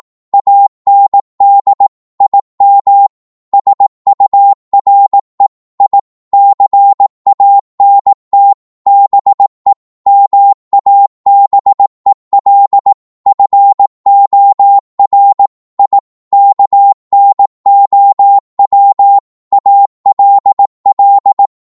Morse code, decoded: AND IM SURE I CANT BE MABEL FOR I KNOW ALL